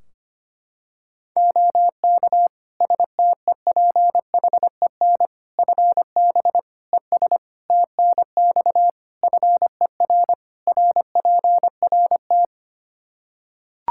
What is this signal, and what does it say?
Morse code, heard as a keyed tone, about 25 wpm, 700 Hz. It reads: OK STEPHEN FB ES TNX FER RPRT
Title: Practice QSO - Exchange 2 - Info - 25wpm